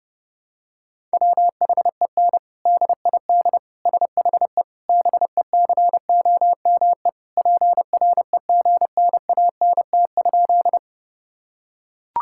Morse code, heard as a keyed tone, about 30 wpm, 700 Hz. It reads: WHEN DID SHE BECOME PREGNANT?
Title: Sentences from Top 1000 Words - Review Part 6 - 30wpm